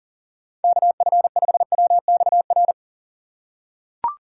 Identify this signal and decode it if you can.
Morse code, heard as a keyed tone, about 37 wpm, 700 Hz. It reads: KF5WXR
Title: USA Call Signs - 3x Character Spacing - 40wpm